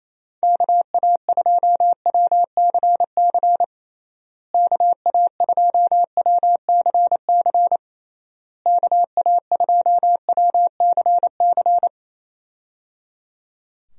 Morse code, heard as a keyed tone, about 28 wpm, 700 Hz. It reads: KA2WCC KA2WCC KA2WCC
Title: USA Call Signs - 6 Characters - Repeated 3x - 28wpm